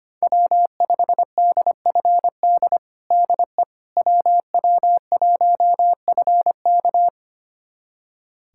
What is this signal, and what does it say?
Morse code, heard as a keyed tone, about 25 wpm, 700 Hz. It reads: W5DFD DE WW1FK
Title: USA Callsign DE Callsign - 5 Characters - 25wpm